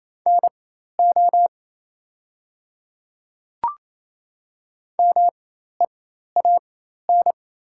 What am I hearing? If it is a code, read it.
Morse code, heard as a keyed tone, about 28 wpm, spaced as for 13 wpm, 690 Hz. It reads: NO MEAN
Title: 4x Character Spacing - Top 300 Words - 28wpm